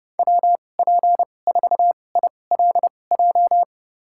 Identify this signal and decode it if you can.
Morse code, heard as a keyed tone, about 30 wpm, spaced as for 22 wpm, 700 Hz. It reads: WP4ILJ
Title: USA Call Signs - 6 Characters - 2x Character Spacing - 30wpm